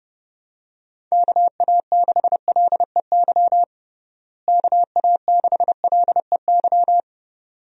Morse code, heard as a keyed tone, about 30 wpm, 700 Hz. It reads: KA6LEY KA6LEY
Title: USA Call Signs - 6 Characters - Repeated 2x - 30wpm